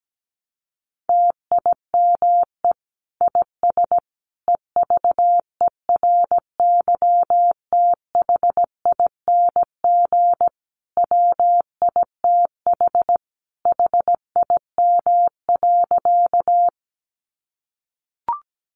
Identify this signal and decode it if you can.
Morse code, heard as a keyed tone, about 17 wpm, 700 Hz. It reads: TIME IS EVERYTHING WITH HIM.
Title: Sentences from Top 700 Words 17wpm